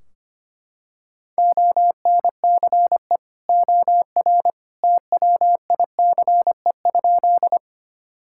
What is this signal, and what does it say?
Morse code, heard as a keyed tone, about 25 wpm, 700 Hz. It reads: ONCE OR TWICE?
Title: Sentences from Top 1600 Words 25wpm